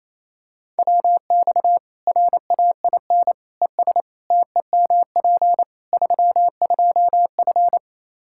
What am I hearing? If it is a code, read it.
Morse code, heard as a keyed tone, about 27 wpm, 700 Hz. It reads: WX RAIN ES TEMP 32F
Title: K1USN SST QSO Practice 28wpm